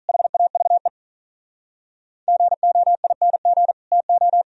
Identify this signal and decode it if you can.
Morse code, heard as a keyed tone, about 43 wpm, 700 Hz. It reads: HAVE GOING TO
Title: Speed-Racing - Sets of 3 Words 35wpm